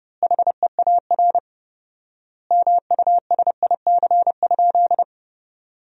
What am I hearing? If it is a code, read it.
Morse code, heard as a keyed tone, about 30 wpm, 700 Hz. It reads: HEAR MUSIC?
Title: 4x Word-Spacing - Sentences from Top 300 Words - 30wpm